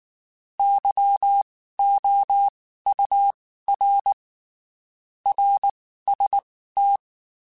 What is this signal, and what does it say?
Morse code, heard as a keyed tone, about 19 wpm, spaced as for 14 wpm, 800 Hz. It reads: YOUR RST
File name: (spaced) / QSOs Spaced-mp3s / WD0FBT 1 @18 sp